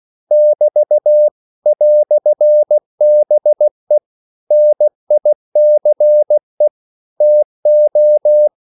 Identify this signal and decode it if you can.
Morse code, heard as a keyed tone, about 16 wpm, 600 Hz. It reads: <BT> "BE NICE TO
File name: CW / Practice / EnglishQuotations0216